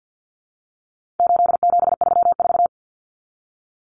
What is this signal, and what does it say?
Morse code, heard as a keyed tone, about 50 wpm, 700 Hz. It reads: 8734
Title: Common Cut Numbers - 4-Digit 50wpm